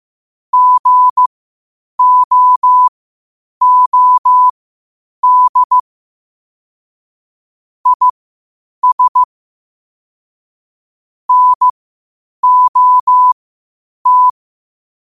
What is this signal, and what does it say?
Morse code, heard as a keyed tone, about 15 wpm, spaced as for 8 wpm, 1000 Hz. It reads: GOOD IS NOT
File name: qotdfiles / 8wpm